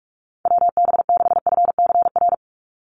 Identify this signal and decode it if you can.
Morse code, heard as a keyed tone, about 44 wpm, 700 Hz. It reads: WB6FCR
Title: USA Call Signs 45wpm